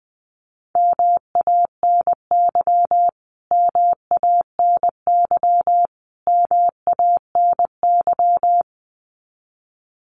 Morse code, heard as a keyed tone, about 20 wpm, 700 Hz. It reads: MANY MANY MANY
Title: Top 300 Words Repeated 20wpm